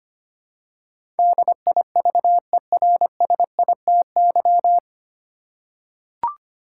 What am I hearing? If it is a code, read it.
Morse code, heard as a keyed tone, about 25 wpm, 700 Hz. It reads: DIVERSITY
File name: Nine Letter Words - 25wpm